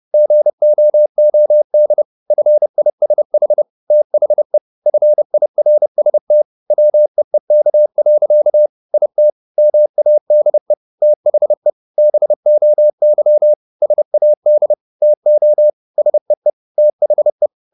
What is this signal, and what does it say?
Morse code, heard as a keyed tone, about 30 wpm, 600 Hz. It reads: GOOD FISH THE FIRST WEEK. IT MADE THE BOY SAD TO SEE THE
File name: the old man and the sea / the old man and the sea 1 30wpm